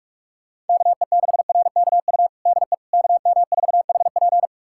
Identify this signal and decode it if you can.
Morse code, heard as a keyed tone, about 41 wpm, 700 Hz. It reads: KE6RKU DE KM4HP
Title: USA Callsign DE Callsign 45wpm